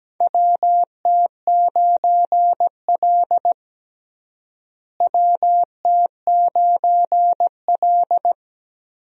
Morse code, heard as a keyed tone, about 17 wpm, 700 Hz. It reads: WT9L WT9L
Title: USA Call Signs - 4 Characters - Repeated 3x - 17wpm